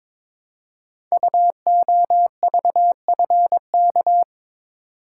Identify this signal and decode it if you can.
Morse code, heard as a keyed tone, about 22 wpm, 700 Hz. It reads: UOVFK
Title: Code Groups - 5 Letters - 22wpm